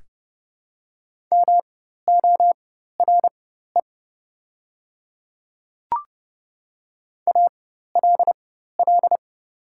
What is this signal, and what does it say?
Morse code, heard as a keyed tone, about 30 wpm, spaced as for 14 wpm, 700 Hz. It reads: MORE ALL
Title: 4x Character Spacing - Top 100 Words - 30wpm